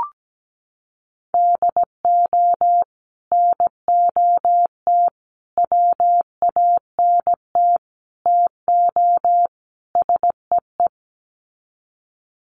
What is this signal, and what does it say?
Morse code, heard as a keyed tone, about 17 wpm, 700 Hz. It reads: DO NOT WANT TO SEE
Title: Sets of 5 Words 17wpm